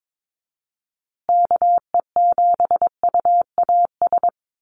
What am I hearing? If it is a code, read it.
Morse code, heard as a keyed tone, about 22 wpm, 700 Hz. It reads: KE7UAS
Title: USA Call Signs Repeated 3x 22wpm